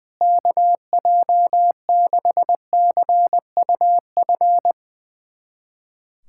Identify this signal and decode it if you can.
Morse code, heard as a keyed tone, about 20 wpm, 700 Hz. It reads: KJ6CUF
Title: USA Callsign DE Callsign - 6 Characters - 20wpm